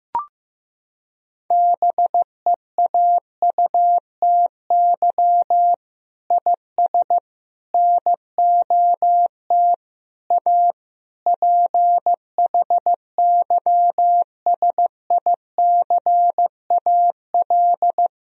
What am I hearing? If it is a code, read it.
Morse code, heard as a keyed tone, about 16 wpm, 700 Hz. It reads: BEAUTY IS NOT A PHYSICAL
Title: Sentences from Top 1000 Words - Review Part 4 - 15wpm